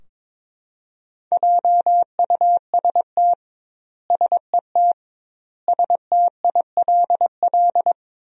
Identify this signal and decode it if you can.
Morse code, heard as a keyed tone, about 22 wpm, 700 Hz. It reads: JUST SET STILL
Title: 2x Word-Spacing - Sentences from Top 300 Words - 22wpm